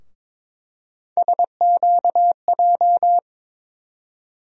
Morse code, heard as a keyed tone, about 21 wpm, 700 Hz. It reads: SQJ
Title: USA Call Sign Postfixes - 3 Letters 22wpm